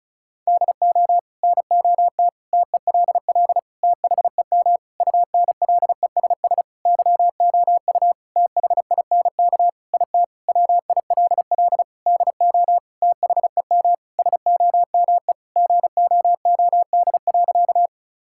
Morse code, heard as a keyed tone, about 36 wpm, 700 Hz. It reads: DO NOT TELL THEM UNLESS YOU THINK IT WILL DO THEM SOME GOOD.
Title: Sentences from Top 1000 Words - Review Part 6 - 35wpm